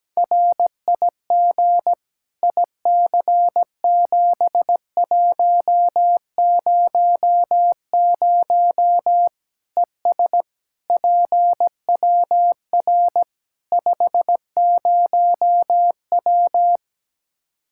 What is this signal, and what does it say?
Morse code, heard as a keyed tone, about 17 wpm, 700 Hz. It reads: RIG IC7100 ES PWR 50W
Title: Practice QSO - Exchange 2 - Info - 17wpm